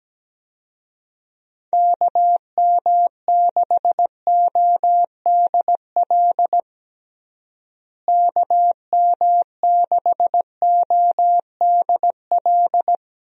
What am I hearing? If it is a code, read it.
Morse code, heard as a keyed tone, about 17 wpm, 700 Hz. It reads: KM6ODL KM6ODL
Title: USA Call Signs - 6 Characters - Repeated 2x - 17wpm